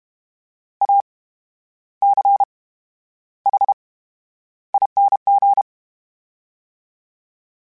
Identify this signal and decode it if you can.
Morse code, heard as a keyed tone, about 32 wpm, spaced as for 11 wpm, 800 Hz. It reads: ACHING
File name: Special INGs / Special INGs-mp3s / VerySlowToFaster Ing @30